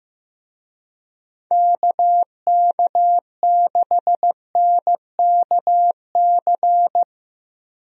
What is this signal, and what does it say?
Morse code, heard as a keyed tone, about 15 wpm, 700 Hz. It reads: KK6NKC
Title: USA Call Signs - 6 Characters - Repeated 3x - 15wpm